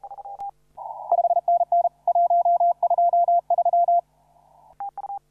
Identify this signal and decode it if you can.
Morse code, heard as a keyed tone, about 30 wpm, 690 Hz. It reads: 5NN 123
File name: CW / RHOB CW 2017 HA6P 0001